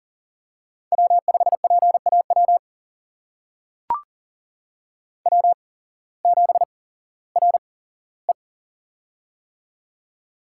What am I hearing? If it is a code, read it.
Morse code, heard as a keyed tone, about 39 wpm, spaced as for 19 wpm, 700 Hz. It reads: W5PAW W7RE
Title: USA Call Signs - 8x Character Spacing - 40wpm